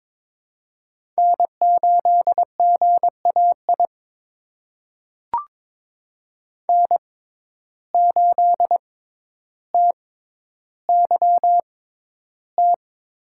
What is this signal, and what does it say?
Morse code, heard as a keyed tone, about 22 wpm, spaced as for 11 wpm, 700 Hz. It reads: N8GAI N8TYT
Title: USA Call Signs - 5 Characters - 6x Character Spacing - 22wpm